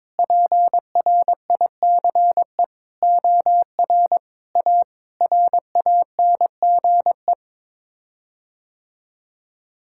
Morse code, heard as a keyed tone, about 22 wpm, 700 Hz. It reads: PRICE OR A RANGE
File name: Sets of 9 Words - 22wpm